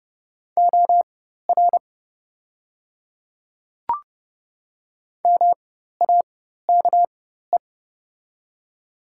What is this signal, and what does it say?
Morse code, heard as a keyed tone, about 30 wpm, spaced as for 14 wpm, 700 Hz. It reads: OR MAKE
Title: 4x Character Spacing - Top 100 Words - 30wpm